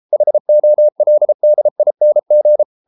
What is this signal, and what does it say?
Morse code, heard as a keyed tone, about 33 wpm, 600 Hz. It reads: HOLDING